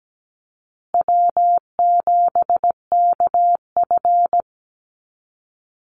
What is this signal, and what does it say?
Morse code, heard as a keyed tone, about 17 wpm, 700 Hz. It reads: W7KF
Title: Call Signs Repeated 3x - 17wpm